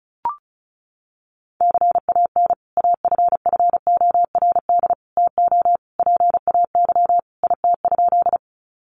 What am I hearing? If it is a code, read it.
Morse code, heard as a keyed tone, about 34 wpm, 700 Hz. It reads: CAN AFFORD TO PAY IT?
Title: Sentences from Top 1100 Words 35wpm